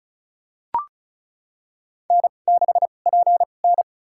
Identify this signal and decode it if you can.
Morse code, heard as a keyed tone, about 36 wpm, spaced as for 25 wpm, 700 Hz. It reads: N6PN
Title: USA Call Signs - 4 Characters - 2x Character Spacing - 35wpm